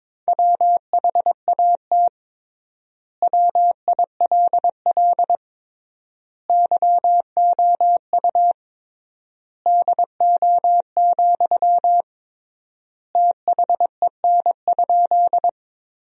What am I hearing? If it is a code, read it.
Morse code, heard as a keyed tone, about 22 wpm, 700 Hz. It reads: WHAT WILL YOU DO, THEN?
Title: 3x Word-Spacing - Sentences from Top 100 Words - 22wpm